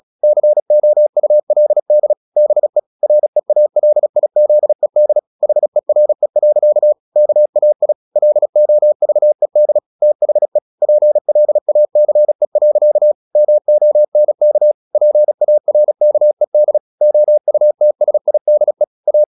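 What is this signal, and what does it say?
Morse code, heard as a keyed tone, about 36 wpm, 600 Hz. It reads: COULD BE REALIZED HERE. KAI LOVED THE PLACE. MONK PARKED OUTSIDE A
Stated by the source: Morse practice at 36 wpm, text of Ken Follett's novel Never, from CW med Gnister.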